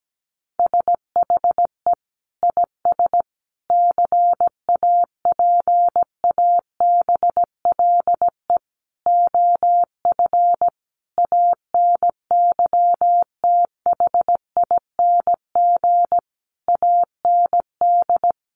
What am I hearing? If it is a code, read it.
Morse code, heard as a keyed tone, about 17 wpm, 700 Hz. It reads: SHE IS CAPABLE OF ANYTHING AND
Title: Sentences from Top 1200 Words 17wpm